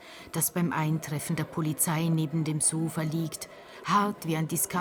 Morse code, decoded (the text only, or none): none